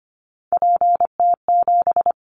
Morse code, heard as a keyed tone, about 25 wpm, 700 Hz. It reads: PT7
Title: Callsign Prefixes for Top 100 DXCC - 25wpm